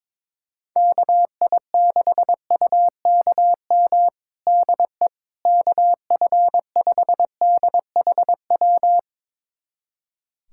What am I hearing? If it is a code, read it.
Morse code, heard as a keyed tone, about 22 wpm, 700 Hz. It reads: KI6UKM DE KF5DHW
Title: USA Callsign DE Callsign - 6 Characters - 22wpm